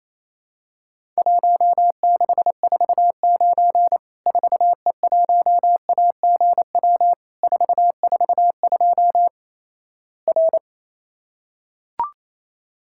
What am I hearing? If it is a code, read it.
Morse code, heard as a keyed tone, about 28 wpm, 700 Hz. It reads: 1649 4E1AGW 442 R
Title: Worked All Europe DX Contest - QTC Exchange - 28wpm